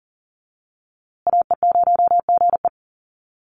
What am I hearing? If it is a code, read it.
Morse code, heard as a keyed tone, about 40 wpm, 700 Hz. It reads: AE0GE
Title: USA Call Signs Repeated 2x 40wpm